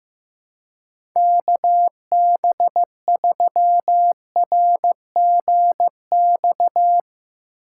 Morse code, heard as a keyed tone, about 15 wpm, 700 Hz. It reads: KB3RGX
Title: USA Call Signs - 6 Characters - 4x Character Spacing - 15wpm